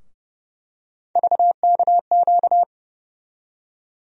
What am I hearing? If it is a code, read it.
Morse code, heard as a keyed tone, about 30 wpm, 700 Hz. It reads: VKQ
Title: USA Call Sign Postfixes - 3 Letters 30wpm